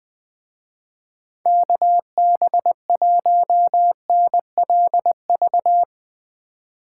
Morse code, heard as a keyed tone, about 19 wpm, 700 Hz. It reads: KB1NLV
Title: Speed Racing - USA Call Signs 17wpm